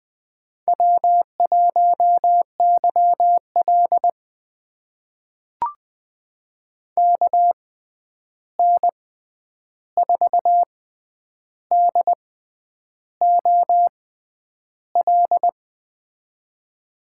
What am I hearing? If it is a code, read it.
Morse code, heard as a keyed tone, about 20 wpm, spaced as for 9 wpm, 700 Hz. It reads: W1YL KN4DOL
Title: USA Call Signs - 6x Character Spacing - 20wpm